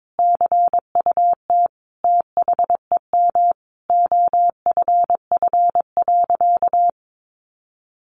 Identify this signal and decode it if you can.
Morse code, heard as a keyed tone, about 22 wpm, 700 Hz. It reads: CUT THEM OFF.
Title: Sentences from Top 1000 Words 22wpm